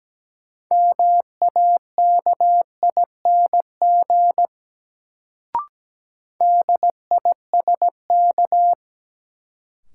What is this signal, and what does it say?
Morse code, heard as a keyed tone, about 17 wpm, 700 Hz. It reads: MAKING DISK
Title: Sentences from Top 1000 Words - Review Part 6 - New Word Drill - 17wpm